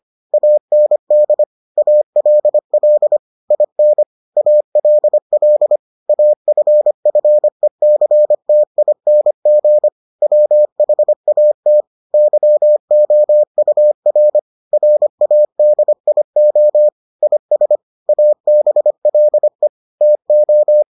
Morse code, heard as a keyed tone, about 25 wpm, 600 Hz. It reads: AND ALL IN ALL AFFECTING WHAT YOUR RADIO IS ABLE TO